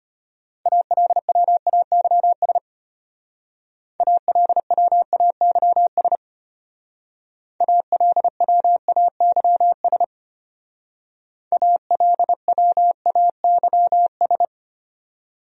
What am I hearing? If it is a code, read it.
Morse code, heard as a keyed tone, about 30 wpm, 700 Hz. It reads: ALWAYS ALWAYS ALWAYS ALWAYS
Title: Speed-Racing Top 300 Words 25wpm